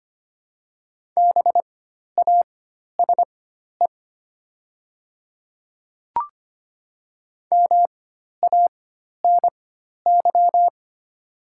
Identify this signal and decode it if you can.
Morse code, heard as a keyed tone, about 25 wpm, spaced as for 12 wpm, 700 Hz. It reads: BASE MANY
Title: 4x Character Spacing - Top 300 Words - 25wpm